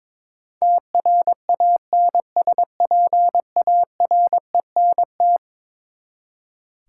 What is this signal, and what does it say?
Morse code, heard as a keyed tone, about 22 wpm, 700 Hz. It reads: TRANSPARENT
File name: practice / 22wpm / Eleven Letter Words - 22wpm